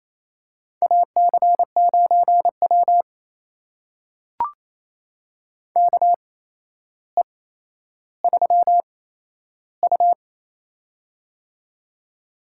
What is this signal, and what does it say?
Morse code, heard as a keyed tone, about 28 wpm, spaced as for 12 wpm, 700 Hz. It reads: AC9W KE3U
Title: USA Call Signs - 4 Characters - 8x Character Spacing - 28wpm